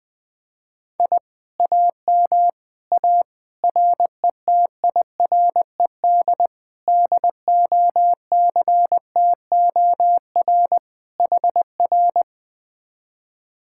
Morse code, heard as a keyed tone, about 20 wpm, 700 Hz. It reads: I AM A RETIRED DOCTOR HR
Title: Speed-Racing Common QSO Phrases - 20wpm